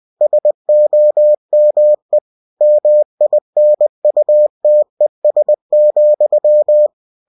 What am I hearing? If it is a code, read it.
Morse code, heard as a keyed tone, about 20 wpm, 600 Hz. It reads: SOME MINUTES,
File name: cw-books / winnie-the-pooh / winnie-20wpm-page-0060